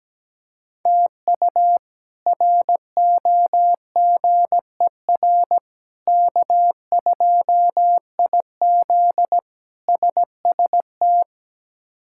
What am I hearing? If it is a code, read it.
Morse code, heard as a keyed tone, about 17 wpm, 700 Hz. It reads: TU ROGER K2IZ SST
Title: K1USN SST QSO Practice - 17wpm